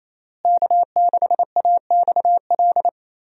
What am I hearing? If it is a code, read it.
Morse code, heard as a keyed tone, about 28 wpm, 700 Hz. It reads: K6AXL
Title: USA Call Signs - 5 Characters - Repeated 3x - 28wpm